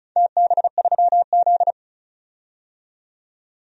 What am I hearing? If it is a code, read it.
Morse code, heard as a keyed tone, about 36 wpm, 700 Hz. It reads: TB3Z
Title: DXCC-100 Call Signs - 4 Characters - 35wpm